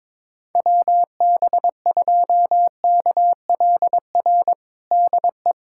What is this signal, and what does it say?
Morse code, heard as a keyed tone, about 22 wpm, 700 Hz. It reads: WB2KLR DE
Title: USA Callsign DE Callsign 22wpm